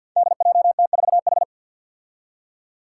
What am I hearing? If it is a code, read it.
Morse code, heard as a keyed tone, about 49 wpm, 700 Hz. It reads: DJT4H
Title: Five Challenging Letter-Number Combinations 50wpm